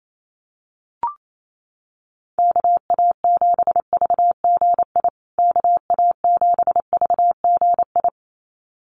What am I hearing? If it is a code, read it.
Morse code, heard as a keyed tone, about 28 wpm, 700 Hz. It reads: KA7VGI KA7VGI
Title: USA Call Signs Repeated 2x 28wpm